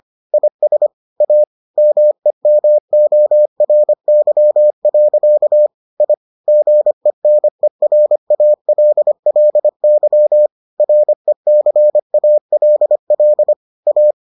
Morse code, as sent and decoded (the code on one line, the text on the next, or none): .. ... / .- / -- . -- --- .-. -.-- .-.-.- / .. / --. -. .-. .- .-.. .-.. -.-- / .-. -.-. .- .-.. .-.. / .-
IS A MEMORY. I GENERALLY RECALL A